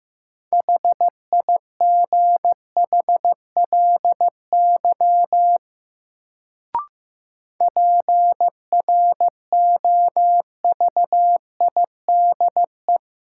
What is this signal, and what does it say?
Morse code, heard as a keyed tone, about 15 wpm, 700 Hz. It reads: HIGHLY PROVIDE
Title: Sentences from Top 1100 Words - New Word Drill Alt - 15wpm